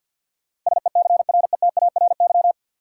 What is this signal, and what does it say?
Morse code, heard as a keyed tone, about 53 wpm, 700 Hz. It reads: SECRETARY
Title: Sentences from Top 1300 Words - New Word Drill Alt - 50wpm